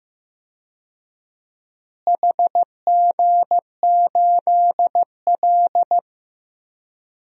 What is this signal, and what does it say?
Morse code, heard as a keyed tone, about 15 wpm, 700 Hz. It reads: HG8L
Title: DXCC-100 Call Signs - 4 Characters - Repeated 3x - 15wpm